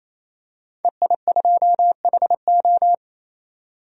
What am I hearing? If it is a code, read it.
Morse code, heard as a keyed tone, about 28 wpm, 700 Hz. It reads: EI2HO
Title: DXCC-100 Call Signs - 5 Characters - 28wpm